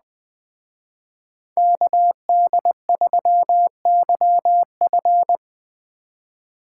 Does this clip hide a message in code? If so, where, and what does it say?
Yes: Morse code from 1.6 to 5.4 s, heard as a keyed tone, about 21 wpm, 700 Hz. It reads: KD3YF